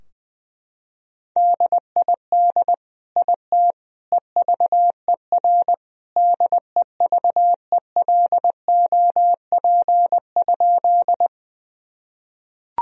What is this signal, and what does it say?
Morse code, heard as a keyed tone, about 20 wpm, 700 Hz. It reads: DID IT EVER DEVELOP?
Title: Sentences from Top 1000 Words - Review Part 2 - 20wpm